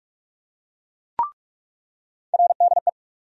Morse code, heard as a keyed tone, about 46 wpm, 700 Hz. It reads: RDE
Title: USA Call Sign Postfixes - 3 Letters 45wpm